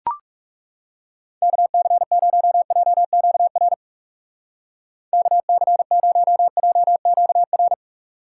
Morse code, heard as a keyed tone, about 38 wpm, 700 Hz. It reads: KC0JQR KC0JQR